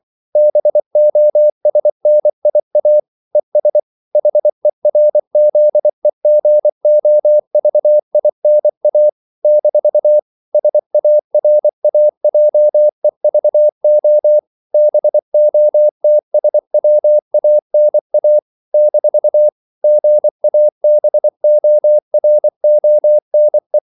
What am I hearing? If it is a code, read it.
Morse code, heard as a keyed tone, about 24 wpm, 600 Hz. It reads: BOSNIA ES HERZEGOVINA - SARAJEVO BOTSWANA - GABORONE
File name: CW / Practice / NationalCapitals24